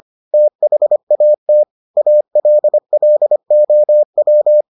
Morse code, heard as a keyed tone, about 25 wpm, 600 Hz. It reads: THAT ALLOW